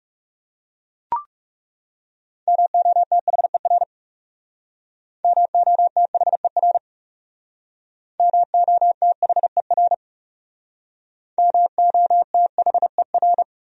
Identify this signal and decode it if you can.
Morse code, heard as a keyed tone, about 37 wpm, 700 Hz. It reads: MOTHER MOTHER MOTHER MOTHER
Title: Speed-Racing Top 500 Words 30wpm